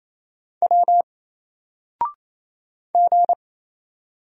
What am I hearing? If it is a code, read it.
Morse code, heard as a keyed tone, about 28 wpm, 700 Hz. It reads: W G
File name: Single-Letter-Number-28wpm